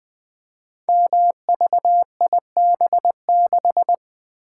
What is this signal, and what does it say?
Morse code, heard as a keyed tone, about 20 wpm, 700 Hz. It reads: MVIB6
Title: Code Groups - 5 Letter:Numbers - 20wpm